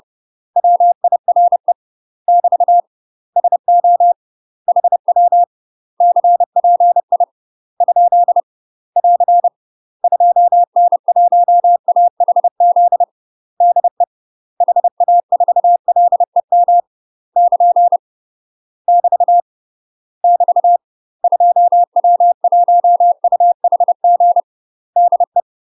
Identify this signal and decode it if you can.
Morse code, heard as a keyed tone, about 30 wpm, 700 Hz. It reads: WIRE <BT> SO HW CPI ? <AR> 2N1AHZ DE HA4LEM <KN> <BT> <BT> 2W1UHG DE
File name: UK MASTER QSOs TRAINING/30WPM